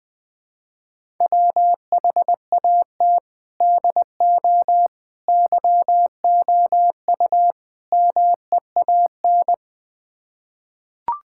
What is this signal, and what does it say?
Morse code, heard as a keyed tone, about 20 wpm, 700 Hz. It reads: WHAT DO YOU MEAN
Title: Build A Phrase - 4-Word Phrases - 20wpm